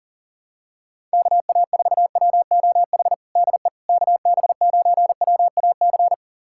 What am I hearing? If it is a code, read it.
Morse code, heard as a keyed tone, about 40 wpm, 700 Hz. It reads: KA4WOH DE KB9WAC
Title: USA Callsign DE Callsign 40wpm